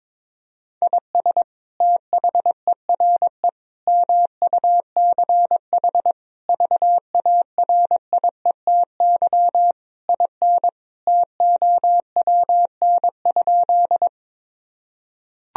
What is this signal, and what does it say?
Morse code, heard as a keyed tone, about 22 wpm, 700 Hz. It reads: IS THERE MUCH VARIETY IN TOWN?
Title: Sentences from Top 1000 Words - Review Part 4 - 22wpm